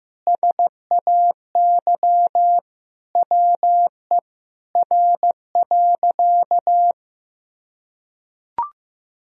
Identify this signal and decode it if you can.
Morse code, heard as a keyed tone, about 15 wpm, 700 Hz. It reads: SAY WE R.
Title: Sentences from Top 200 Words and Abbreviations 15wpm